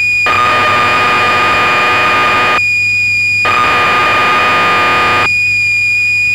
Cassette data (Kansas City type).
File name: magazines / Acorn User / 1984-12 BBC Programs